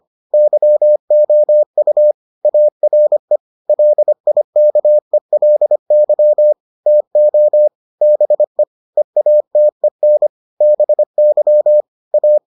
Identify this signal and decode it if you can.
Morse code, heard as a keyed tone, about 25 wpm, 600 Hz. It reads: YOU ARE LIKELY TO BE EATEN BY A